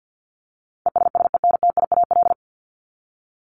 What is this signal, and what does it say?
Morse code, heard as a keyed tone, about 50 wpm, 700 Hz. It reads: ESSENTIAL
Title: LY Suffix - Top 100 Words with Base Words 50wpm